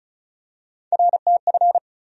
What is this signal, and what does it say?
Morse code, heard as a keyed tone, about 36 wpm, 700 Hz. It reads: RTF
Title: USA Call Sign Postfixes - 3 Letters 35wpm